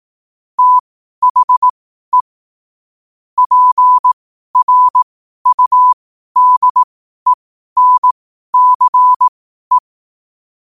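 Morse code, heard as a keyed tone, about 18 wpm, spaced as for 12 wpm, 1000 Hz. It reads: THE PRUDENCE
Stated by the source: 12 wpm Morse code.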